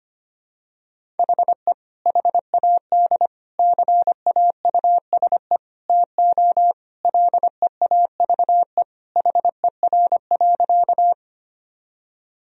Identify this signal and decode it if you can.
Morse code, heard as a keyed tone, about 25 wpm, 700 Hz. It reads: HE HAD CAUSE TO LEAVE HER.
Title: Sentences from Top 1000 Words - Review Part 1 - 25wpm